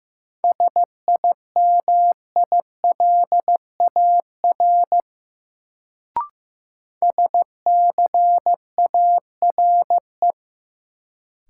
Morse code, heard as a keyed tone, about 15 wpm, 700 Hz. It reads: SIMILAR SCARE
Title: Sentences from Top 1000 Words - Review Part 5 - New Word Drill - 15wpm